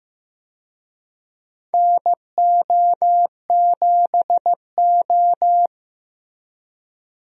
Morse code, heard as a keyed tone, about 15 wpm, 700 Hz. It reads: NO7O
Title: USA Call Signs - 4 Characters - 15wpm